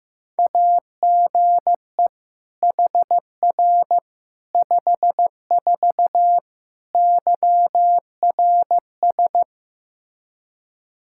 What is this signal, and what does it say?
Morse code, heard as a keyed tone, about 15 wpm, 700 Hz. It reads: AGE HR 54 YRS
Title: Practice QSO - Exchange 3 - More Info - 15wpm